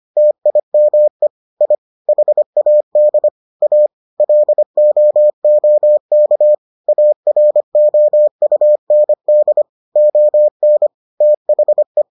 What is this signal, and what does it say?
Morse code, heard as a keyed tone, about 25 wpm, 600 Hz. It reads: TIME I HAD A LOOK AROUND ON THE